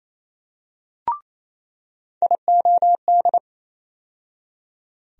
Morse code, heard as a keyed tone, about 28 wpm, 700 Hz. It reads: IOD
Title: USA Call Sign Postfixes - 3 Letters 28wpm